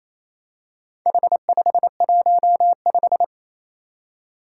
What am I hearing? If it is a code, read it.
Morse code, heard as a keyed tone, about 27 wpm, 700 Hz. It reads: H515
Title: Four Challenging Letter-Number Combinations 28wpm